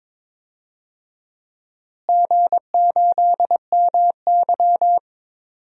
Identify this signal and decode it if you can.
Morse code, heard as a keyed tone, about 22 wpm, 700 Hz. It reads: G8MY
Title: DXCC-100 Call Signs - 4 Characters - Repeated 3x - 22wpm